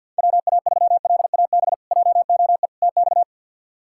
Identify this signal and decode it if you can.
Morse code, heard as a keyed tone, about 48 wpm, 700 Hz. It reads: WA3LAB JOE TX
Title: K1USN SST QSO Practice - 50wpm